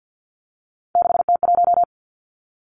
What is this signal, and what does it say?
Morse code, heard as a keyed tone, about 50 wpm, 700 Hz. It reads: 6T1
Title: Three Challenging Letter-Number Combinations 50wpm